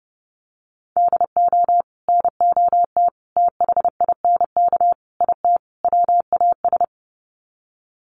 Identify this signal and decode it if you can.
Morse code, heard as a keyed tone, about 30 wpm, 700 Hz. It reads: DO NOT THINK IT WAS
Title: Sets of 5 Words Encore 30wpm